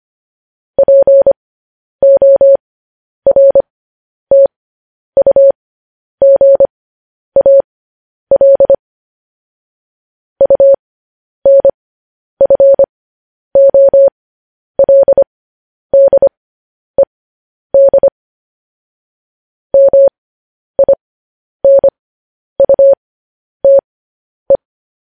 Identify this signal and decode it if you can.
Morse code, heard as a keyed tone, about 25 wpm, spaced as for 10 wpm, 550 Hz. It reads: PORTUGAL UNFOLDED MINUTE
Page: news headlines in morse code at 10 wpm